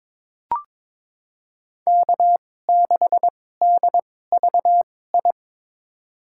Morse code, heard as a keyed tone, about 21 wpm, spaced as for 16 wpm, 700 Hz. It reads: K6DVI
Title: USA Call Signs - 5 Characters - 2x Character Spacing - 22wpm